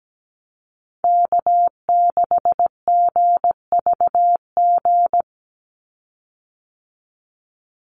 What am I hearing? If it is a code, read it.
Morse code, heard as a keyed tone, about 17 wpm, 700 Hz. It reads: K6GVG
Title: Call Signs Repeated 3x - 17wpm